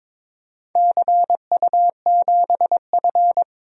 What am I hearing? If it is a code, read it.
Morse code, heard as a keyed tone, about 22 wpm, 700 Hz. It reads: CU7F
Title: DXCC-100 Call Signs - 4 Characters - Repeated 2x - 22wpm